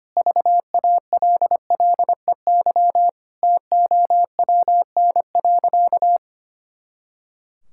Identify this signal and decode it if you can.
Morse code, heard as a keyed tone, about 25 wpm, 700 Hz. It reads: VALLEY TOWN.
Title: Pangram Challenge v2 - 25wpm